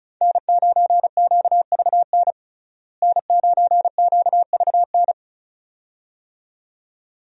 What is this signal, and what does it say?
Morse code, heard as a keyed tone, about 34 wpm, 700 Hz. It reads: N9QVN N9QVN
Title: USA Call Signs - 5 Characters - Repeated 2x - 35wpm